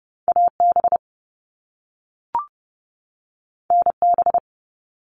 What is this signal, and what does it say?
Morse code, heard as a keyed tone, about 30 wpm, 700 Hz. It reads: AB NB
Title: Canadian Provinces and Territories 30wpm